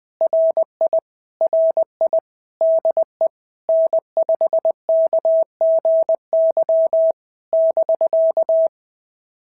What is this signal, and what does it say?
Morse code, heard as a keyed tone, about 20 wpm, 650 Hz. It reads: RI RI DE N5KGY <BK>
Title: POTA QSO Practice - 20wpm